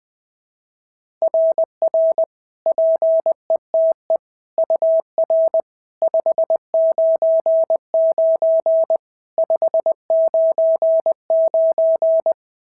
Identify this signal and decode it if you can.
Morse code, heard as a keyed tone, about 18 wpm, 650 Hz. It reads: RR PETE UR 599 599
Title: SOTA QSO Practice 20wpm